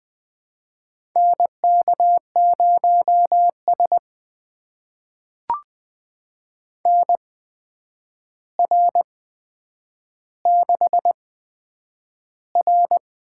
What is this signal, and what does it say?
Morse code, heard as a keyed tone, about 20 wpm, spaced as for 9 wpm, 700 Hz. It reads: NK0S NR6R
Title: USA Call Signs - 4 Characters - 8x Character Spacing - 20wpm